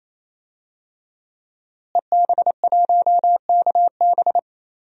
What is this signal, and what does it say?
Morse code, heard as a keyed tone, about 28 wpm, 700 Hz. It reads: EB1KB
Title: DXCC-100 Call Signs - 5 Characters - Repeated 3x - 28wpm